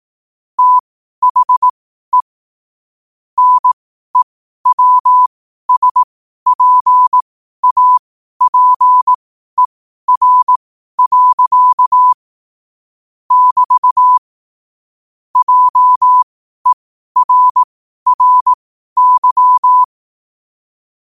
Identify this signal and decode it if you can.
Morse code, heard as a keyed tone, about 18 wpm, spaced as for 12 wpm, 1000 Hz. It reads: THE NEWSPAPER. <BT> JERRY
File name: qotdfiles / 12wpm